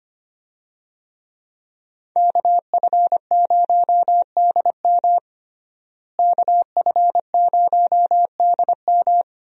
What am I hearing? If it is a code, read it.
Morse code, heard as a keyed tone, about 25 wpm, 700 Hz. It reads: KF0DM KF0DM
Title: USA Call Signs - 5 Characters - Repeated 2x - 25wpm